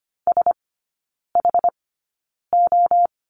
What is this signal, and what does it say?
Morse code, heard as a keyed tone, about 25 wpm, spaced as for 9 wpm, 690 Hz. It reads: SHO
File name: Lesson 14e - Number 2 - Three Characters 25wpm